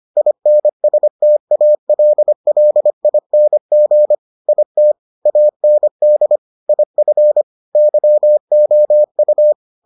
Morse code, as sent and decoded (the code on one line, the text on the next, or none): .. -. ... - .- .-.. .-.. .. -. --. / .. - / .- -. -.. / .. ..-. / -.-- --- ..-
INSTALLING IT AND IF YOU